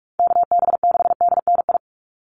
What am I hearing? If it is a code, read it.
Morse code, heard as a keyed tone, about 44 wpm, 700 Hz. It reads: KB6DNI
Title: USA Call Signs 45wpm